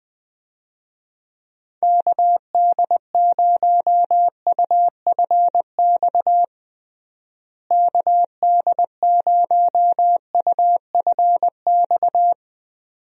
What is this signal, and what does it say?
Morse code, heard as a keyed tone, about 21 wpm, 700 Hz. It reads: KD0UFX KD0UFX
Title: USA Call Signs - 6 Characters - Repeated 2x - 20wpm